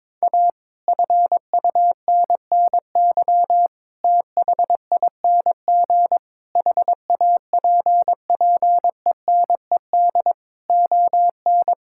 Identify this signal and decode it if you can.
Morse code, heard as a keyed tone, about 22 wpm, 700 Hz. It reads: A FUNNY THING HAPPENED ON
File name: Sets of 9 Words - 22wpm